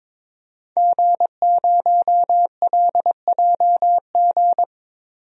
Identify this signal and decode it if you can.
Morse code, heard as a keyed tone, about 22 wpm, 700 Hz. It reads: G0LJG
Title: DXCC-100 Call Signs - 5 Characters - 22wpm